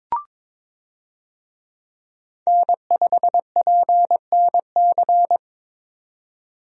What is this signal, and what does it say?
Morse code, heard as a keyed tone, about 22 wpm, 700 Hz. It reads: N5PNC
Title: USA Call Signs - 5 Characters - 22wpm